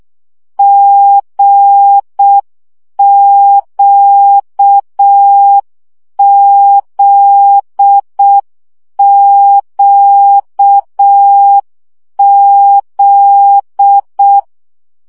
Morse code, heard as a keyed tone, about 6 wpm, 800 Hz. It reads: GQZQZ
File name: CW gyakorlas g z q 5 baud